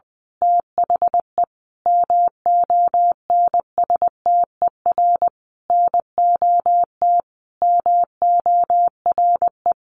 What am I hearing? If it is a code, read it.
Morse code, heard as a keyed tone, about 20 wpm, 700 Hz. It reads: THE MONSTER NOT MORE
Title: A Sound Of Thunder 20wpm